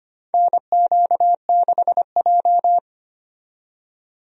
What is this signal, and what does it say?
Morse code, heard as a keyed tone, about 25 wpm, 700 Hz. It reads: NQ6J
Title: USA Call Signs - 4 Characters - 25wpm